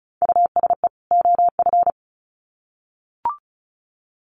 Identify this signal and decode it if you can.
Morse code, heard as a keyed tone, about 34 wpm, 700 Hz. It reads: USE OF
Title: Sets of 2 Words 35wpm